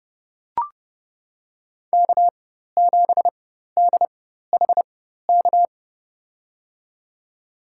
Morse code, heard as a keyed tone, about 32 wpm, spaced as for 14 wpm, 700 Hz. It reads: K7DHK
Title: USA Call Signs - 5 Characters - 4x Character Spacing - 30wpm